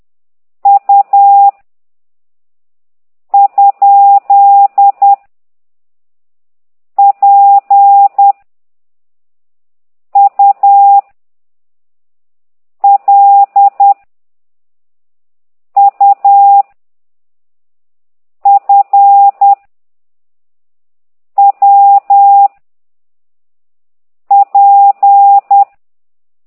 Morse code, heard as a keyed tone, about 10 wpm, 800 Hz. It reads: U ? P U L U F W P
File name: ct2idw / exerc9